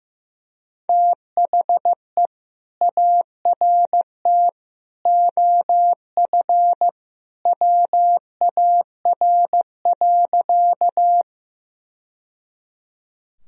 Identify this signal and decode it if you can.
Morse code, heard as a keyed tone, about 15 wpm, 700 Hz. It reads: THE ART OF WAR.
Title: Sentences from Top 1000 Words - Review Part 3 - 15wpm